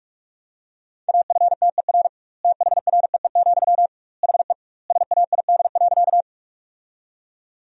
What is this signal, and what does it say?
Morse code, heard as a keyed tone, about 41 wpm, 700 Hz. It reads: AFTER THREE, HE SAID.
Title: 2x Word-Spacing - Sentences from Top 200 Words - 45wpm